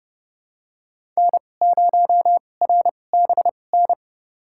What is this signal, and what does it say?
Morse code, heard as a keyed tone, about 30 wpm, spaced as for 22 wpm, 700 Hz. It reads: N0RBN
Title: USA Call Signs - 5 Characters - 2x Character Spacing - 30wpm